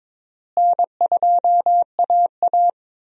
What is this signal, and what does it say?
Morse code, heard as a keyed tone, about 22 wpm, 700 Hz. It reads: N2AA
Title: USA Call Signs - 4 Characters - Repeated 3x - 22wpm